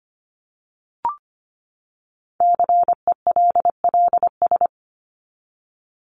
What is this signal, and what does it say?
Morse code, heard as a keyed tone, about 25 wpm, 700 Hz. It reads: CELLS
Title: ES:S Suffix - Top 100 Words with Base Words 25wpm